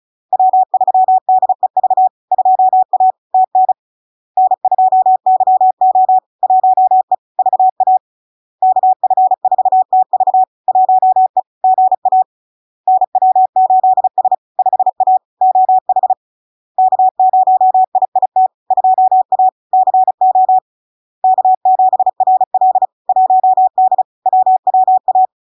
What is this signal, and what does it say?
Morse code, heard as a keyed tone, about 35 wpm, 750 Hz. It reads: W3DEV 2A TN N2YO 1E VA KF4TV 1E GA NW8S 5A OH K0IIT 2A CO K7RL 1D WWA
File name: LARK-FD-2022-Log-in-CW-35wpm-750-Hz